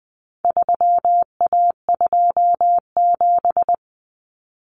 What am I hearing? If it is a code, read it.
Morse code, heard as a keyed tone, about 20 wpm, 700 Hz. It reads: 3A27
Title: Common Cut Numbers - 4-Digit 20wpm